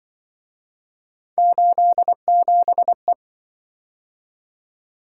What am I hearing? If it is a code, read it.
Morse code, heard as a keyed tone, about 24 wpm, 700 Hz. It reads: 87E
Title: Speed-Racing - Three Letter-Number Combinations - 20wpm